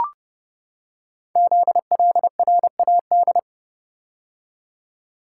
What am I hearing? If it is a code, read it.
Morse code, heard as a keyed tone, about 29 wpm, 700 Hz. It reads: ZLRAD
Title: Code Groups - 5 Letter:Numbers - 30wpm